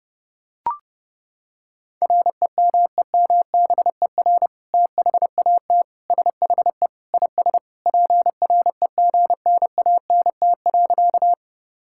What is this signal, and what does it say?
Morse code, heard as a keyed tone, about 28 wpm, 700 Hz. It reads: REMEMBER THAT SHE IS PREGNANT.
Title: Sentences from Top 1000 Words - Review Part 6 - 30wpm